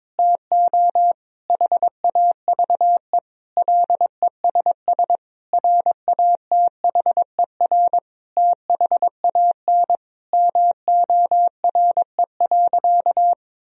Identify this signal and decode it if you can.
Morse code, heard as a keyed tone, about 22 wpm, 700 Hz. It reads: TO HAVE LESS RATHER THAN MORE.
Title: Sentences from Top 1000 Words - Review Part 4 - 22wpm